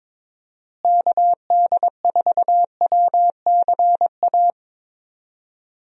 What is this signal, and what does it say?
Morse code, heard as a keyed tone, about 22 wpm, 700 Hz. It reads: KD4WCA
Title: USA Call Signs - 6 Characters - Repeated 2x - 22wpm